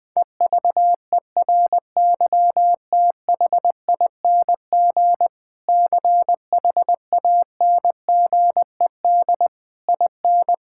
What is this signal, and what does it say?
Morse code, heard as a keyed tone, about 20 wpm, 700 Hz. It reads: EVERYTHING CHANGED IN
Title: Sentences with 6-Word Phrases - 20wpm